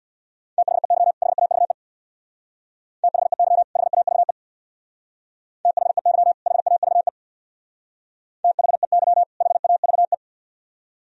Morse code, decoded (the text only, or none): THEY HAVE THEY HAVE THEY HAVE THEY HAVE